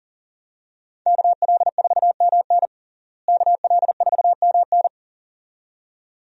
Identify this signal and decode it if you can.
Morse code, heard as a keyed tone, about 37 wpm, 700 Hz. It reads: KL4MN KL4MN
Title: USA Call Signs - 5 Characters - Repeated 2x - 40wpm